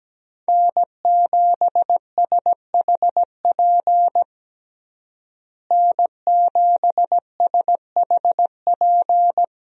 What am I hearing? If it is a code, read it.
Morse code, heard as a keyed tone, about 17 wpm, 700 Hz. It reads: N7SHP N7SHP
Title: USA Call Signs - 5 Characters - Repeated 2x - 17wpm